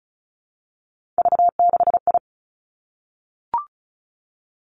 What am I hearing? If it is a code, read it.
Morse code, heard as a keyed tone, about 36 wpm, 700 Hz. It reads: V6I
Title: Three Challenging Letter-Number Combinations 35wpm